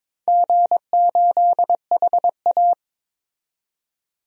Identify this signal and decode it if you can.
Morse code, heard as a keyed tone, about 22 wpm, 700 Hz. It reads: G8HA
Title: DXCC-100 Call Signs - 4 Characters - 22wpm